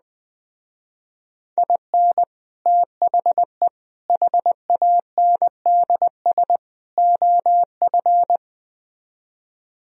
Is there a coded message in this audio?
Yes: Morse code from 1.6 to 8.4 s, heard as a keyed tone, about 20 wpm, 700 Hz. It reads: IN THE HANDS OF